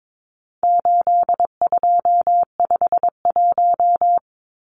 Morse code, heard as a keyed tone, about 22 wpm, 700 Hz. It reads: 8251
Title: Numbers - 4-Digit 22wpm